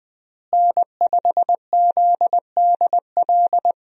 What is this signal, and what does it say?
Morse code, heard as a keyed tone, about 20 wpm, 700 Hz. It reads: N5ZDL
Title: USA Call Signs - 5 Characters - Repeated 3x - 20wpm